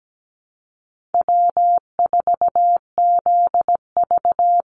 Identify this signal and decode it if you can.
Morse code, heard as a keyed tone, about 17 wpm, 700 Hz. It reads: W4ZV
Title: NA SOTA Chaser Callsigns 17wpm